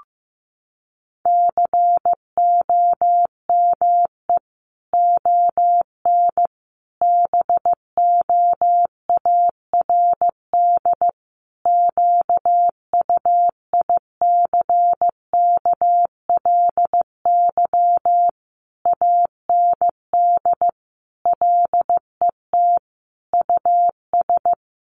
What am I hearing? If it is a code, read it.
Morse code, heard as a keyed tone, about 15 wpm, 700 Hz. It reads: COME ON BOARD QUICKLY AND LET US
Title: Sentences from Top 1300 Words 15wpm